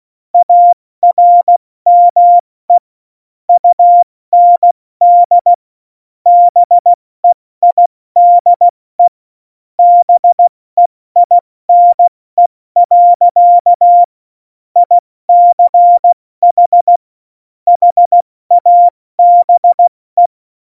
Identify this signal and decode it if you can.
Morse code, heard as a keyed tone, about 16 wpm, 700 Hz. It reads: ARME UND BEIDE BEINE. ICH HABE
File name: Morsekurs / Witze / Witz 01. Dezember 2020, 70 F80